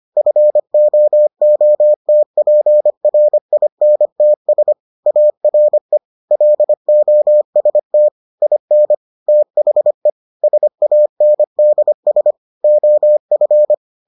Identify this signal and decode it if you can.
Morse code, heard as a keyed tone, about 25 wpm, 600 Hz. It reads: FOOTPRINTS ARE LOST IN THE SANDS OF